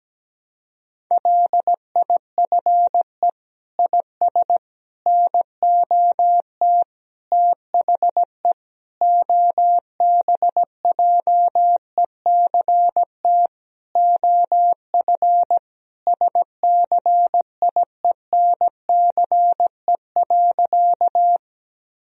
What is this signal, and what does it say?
Morse code, heard as a keyed tone, about 17 wpm, 700 Hz. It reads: LIFE IS NOT THE OBJECT OF SCIENCE.
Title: Sentences from Top 1000 Words - Review Part 3 - 17wpm